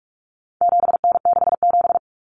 Morse code, heard as a keyed tone, about 44 wpm, 700 Hz. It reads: 7N67
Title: Common Cut Numbers - 4-Digit 45wpm